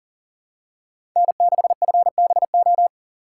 Morse code, heard as a keyed tone, about 39 wpm, 700 Hz. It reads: N6FBO
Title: USA Call Signs - 5 Characters - 2x Character Spacing - 40wpm